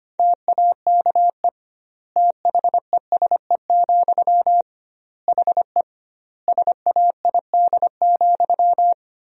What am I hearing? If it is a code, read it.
Morse code, heard as a keyed tone, about 25 wpm, 700 Hz. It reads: TAKE THESE, HE SAID,
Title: 2x Word-Spacing - Sentences from Top 300 Words - 25wpm